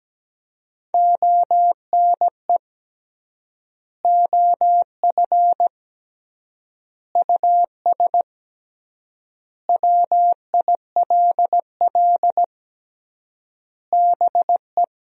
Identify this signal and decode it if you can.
Morse code, heard as a keyed tone, about 17 wpm, 700 Hz. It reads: ONE OF US WILL BE
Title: 3x Word-Spacing - Sentences from Top 300 Words - 17wpm